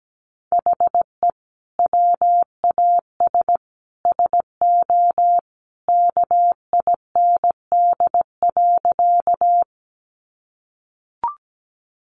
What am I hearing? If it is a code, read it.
Morse code, heard as a keyed tone, about 17 wpm, 700 Hz. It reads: HE WAS SO KIND.
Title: Sentences from Top 200 Words 17wpm